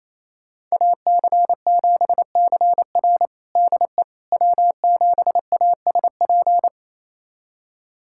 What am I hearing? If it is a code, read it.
Morse code, heard as a keyed tone, about 28 wpm, 700 Hz. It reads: AC7CR DE W7ASP
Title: USA Callsign DE Callsign - 5 Characters - 28wpm